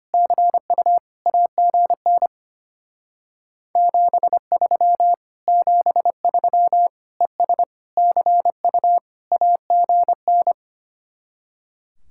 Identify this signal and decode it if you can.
Morse code, heard as a keyed tone, about 32 wpm, 700 Hz. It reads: CU AGN 73 73 ES CU AGN